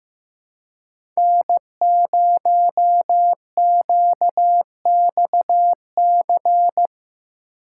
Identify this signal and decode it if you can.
Morse code, heard as a keyed tone, about 15 wpm, 700 Hz. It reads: N0QXC
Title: USA Call Signs - 5 Characters - 4x Character Spacing - 15wpm